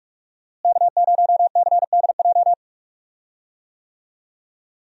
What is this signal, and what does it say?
Morse code, heard as a keyed tone, about 44 wpm, 700 Hz. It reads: K0CDJ
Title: USA Call Signs - 5 Characters - Repeated 2x - 45wpm